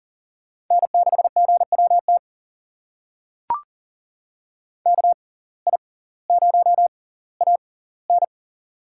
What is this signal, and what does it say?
Morse code, heard as a keyed tone, about 40 wpm, spaced as for 20 wpm, 700 Hz. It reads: N6GWT KI0AN
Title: USA Call Signs - 5 Characters - 6x Character Spacing - 40wpm